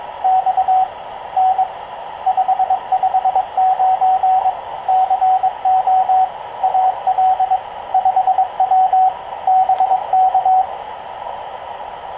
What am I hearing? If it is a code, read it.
Morse code, heard as a keyed tone, about 22 wpm, 750 Hz, with radio noise. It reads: X N 559 CO AL 5W BK
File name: k0frp ki0ii wa8bxn fox utc